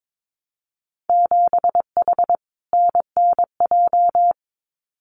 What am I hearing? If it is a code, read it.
Morse code, heard as a keyed tone, about 22 wpm, 700 Hz. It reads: 7H NNJ
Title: Winter Field Day Exchanges Encore 22wpm